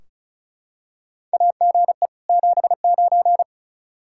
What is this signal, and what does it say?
Morse code, heard as a keyed tone, about 36 wpm, 700 Hz. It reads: AGE 79
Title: K1USN SST QSO Practice 35wpm